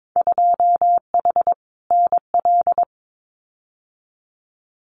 Morse code, heard as a keyed tone, about 22 wpm, 700 Hz. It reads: 2H NL
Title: Winter Field Day Exchanges Encore 22wpm